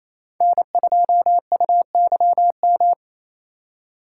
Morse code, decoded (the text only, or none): N2UYM